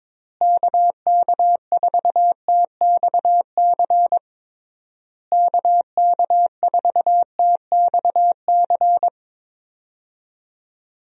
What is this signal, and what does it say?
Morse code, heard as a keyed tone, about 22 wpm, 700 Hz. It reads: KK4TXC KK4TXC
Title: USA Call Signs - 6 Characters - Repeated 3x - 22wpm